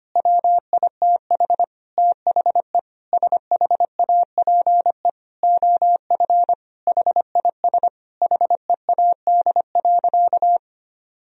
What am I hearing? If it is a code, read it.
Morse code, heard as a keyed tone, about 25 wpm, 700 Hz. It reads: WITH THE SHAPE OF HIS HEAD.
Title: Sentences from Top 1000 Words - Review Part 3 - 25wpm